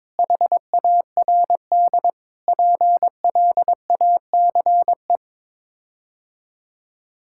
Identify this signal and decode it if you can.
Morse code, heard as a keyed tone, about 22 wpm, 700 Hz. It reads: HARD PLACE
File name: practice / 22wpm / Sets of 8 Words - 22wpm